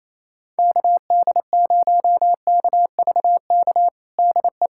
Morse code, heard as a keyed tone, about 28 wpm, 700 Hz. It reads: KD0KVK DE
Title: USA Callsign DE Callsign - 6 Characters - 28wpm